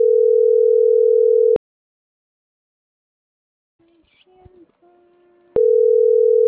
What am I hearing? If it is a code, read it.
Ringback tone (North America), 440 + 480 Hz.